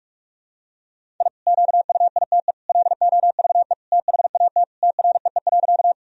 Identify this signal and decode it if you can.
Morse code, heard as a keyed tone, about 46 wpm, 700 Hz. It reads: I QUITE LOVE THAT TREE.
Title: Sentences from Top 1000 Words - Review Part 1 - 45wpm